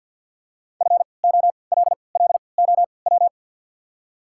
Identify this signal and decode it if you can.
Morse code, heard as a keyed tone, about 49 wpm, spaced as for 28 wpm, 690 Hz. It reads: FOLLOW
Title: 3x Character Spacing - Top 300 Words - 50wpm